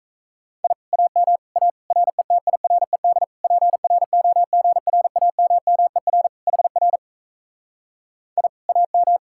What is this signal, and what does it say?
Morse code, heard as a keyed tone, about 38 wpm, 700 Hz. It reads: I AM A RETIRED PROGRAMMER HR I AM